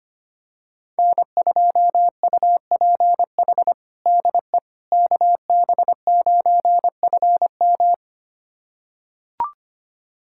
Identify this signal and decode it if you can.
Morse code, heard as a keyed tone, about 25 wpm, 700 Hz. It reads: N2UPH DE KB9FM
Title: USA Callsign DE Callsign - 5 Characters - 25wpm